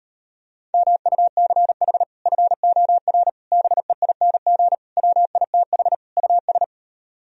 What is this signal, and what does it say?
Morse code, heard as a keyed tone, about 33 wpm, 700 Hz. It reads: MUCH FOR BEING WITH US